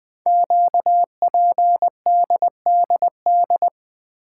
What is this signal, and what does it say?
Morse code, heard as a keyed tone, about 20 wpm, 700 Hz. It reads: QPDDD
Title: Code Groups - 5 Letters - 20wpm